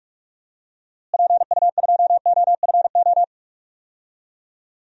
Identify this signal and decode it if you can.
Morse code, heard as a keyed tone, about 43 wpm, 700 Hz. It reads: PU2OFO
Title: DXCC-100 Call Signs - 6 Characters - Repeated 3x - 45wpm